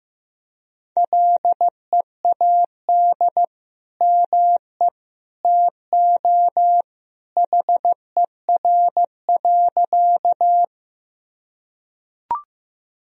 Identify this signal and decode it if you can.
Morse code, heard as a keyed tone, about 15 wpm, 700 Hz. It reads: LEAD ME TO HER.
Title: Sentences from Top 1000 Words - Review Part 2 - 15wpm